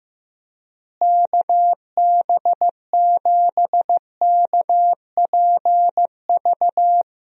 Morse code, heard as a keyed tone, about 15 wpm, 700 Hz. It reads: KB7KPV
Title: USA Call Signs - 6 Characters - Repeated 3x - 15wpm